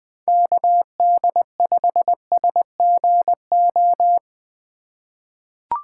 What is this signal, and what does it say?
Morse code, heard as a keyed tone, about 20 wpm, 700 Hz. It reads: KD5SGO
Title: USA Call Signs - 6 Characters - 3x Character Spacing - 20wpm